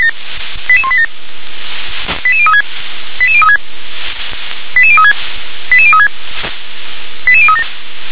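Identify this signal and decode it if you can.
Five-tone selective call (ZVEI), heard as quick runs of five tones, tones of about 70 ms. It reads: 88187 88036 88036 88036 88036 88037
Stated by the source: ZVEI five-tone selective call, address 88182, received on an SDR.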